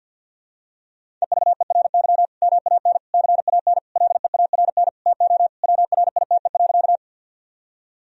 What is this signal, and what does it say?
Morse code, heard as a keyed tone, about 52 wpm, 700 Hz. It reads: EVERY MAN CAN LEARN TO WRITE.
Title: Sentences from Top 1000 Words - Review Part 1 - 50wpm